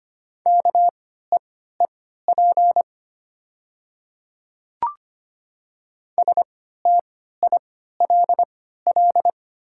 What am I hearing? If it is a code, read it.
Morse code, heard as a keyed tone, about 25 wpm, spaced as for 14 wpm, 690 Hz. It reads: KEEP STILL
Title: 3x Character Spacing - Top 500 Words - 25wpm